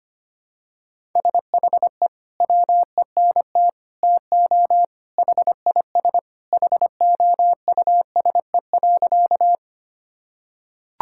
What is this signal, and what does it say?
Morse code, heard as a keyed tone, about 25 wpm, 700 Hz. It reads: SHE WENT TO HIS HOUSE.
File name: practice / 25wpm / Sentences with 2-Word Phrases - 25wpm